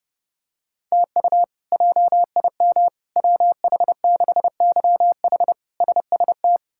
Morse code, heard as a keyed tone, about 30 wpm, 700 Hz. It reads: TU JIM WH6YH SST
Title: K1USN SST QSO Practice - 30wpm